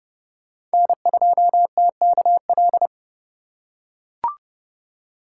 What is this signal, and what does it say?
Morse code, heard as a keyed tone, about 30 wpm, 700 Hz. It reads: N2TKL
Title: USA Call Signs - 5 Characters - 3x Character Spacing - 30wpm